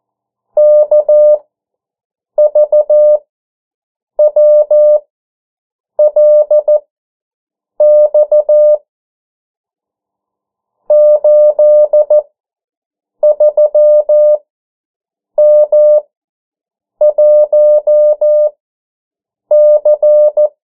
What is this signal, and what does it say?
Morse code, heard as a keyed tone, about 14 wpm, spaced as for 7 wpm, 600 Hz. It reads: KVWLX 83M1C
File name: morse mp3 / 09 urok 8